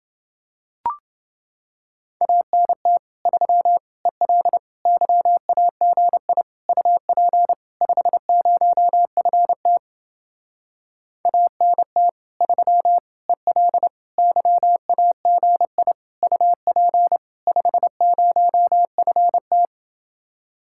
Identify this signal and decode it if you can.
Morse code, heard as a keyed tone, about 29 wpm, 700 Hz. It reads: ANT 3 EL YAGI UP 50FT ANT 3 EL YAGI UP 50FT